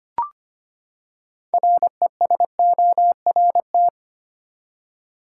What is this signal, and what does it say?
Morse code, heard as a keyed tone, about 25 wpm, 700 Hz. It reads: RESORT
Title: Six Letter Words - 25wpm